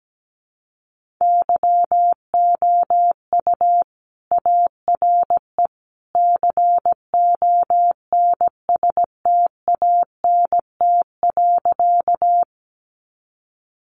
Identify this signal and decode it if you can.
Morse code, heard as a keyed tone, about 17 wpm, 700 Hz. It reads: YOU ARE CONSTANT.
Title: Sentences from Top 1500 Words 17wpm